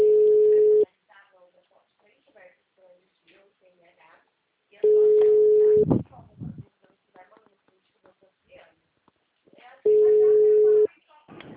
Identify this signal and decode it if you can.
Ringback tone (Europe, 425 Hz), with speech.
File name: out- -2008- - - .26678